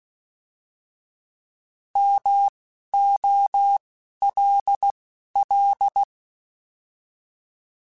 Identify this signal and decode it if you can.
Morse code, heard as a keyed tone, about 16 wpm, spaced as for 11 wpm, 800 Hz. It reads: MOLL
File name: Alphabetical / Alphabetical mp3s / Ms @10